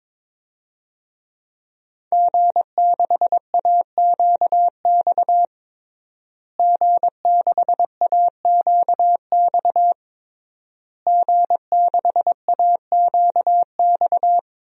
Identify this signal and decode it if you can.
Morse code, heard as a keyed tone, about 22 wpm, 700 Hz. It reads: G6AQX G6AQX G6AQX
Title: DXCC-100 Call Signs - 5 Characters - Repeated 3x - 22wpm